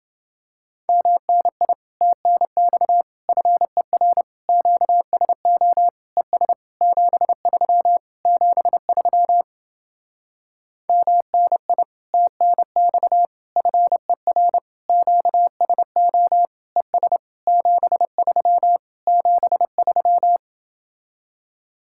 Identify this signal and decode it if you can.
Morse code, heard as a keyed tone, about 29 wpm, 700 Hz. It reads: MNI TNX FER QSO ES 73 73 MNI TNX FER QSO ES 73 73